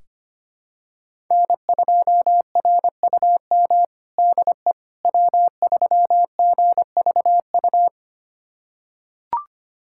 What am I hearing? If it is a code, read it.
Morse code, heard as a keyed tone, about 25 wpm, 700 Hz. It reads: N2RUM DE W3GVU
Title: USA Callsign DE Callsign - 5 Characters - 25wpm